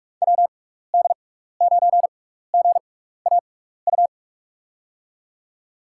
Morse code, heard as a keyed tone, about 44 wpm, spaced as for 15 wpm, 700 Hz. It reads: WD9GAU
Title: USA Call Signs - 6 Characters - 6x Character Spacing - 45wpm